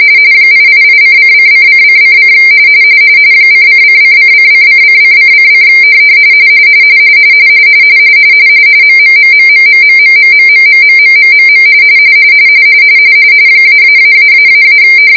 RTTY, about 45.45 baud. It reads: 888:::::::: ::::::::::::: MMMMMMMMMMMMMMMMM -------